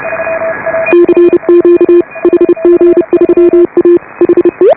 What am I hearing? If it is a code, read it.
Morse code, heard as a keyed tone, about 30 wpm, 340 Hz. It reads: CQ HG3A HE
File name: AUDIO2018 / HA5BA CW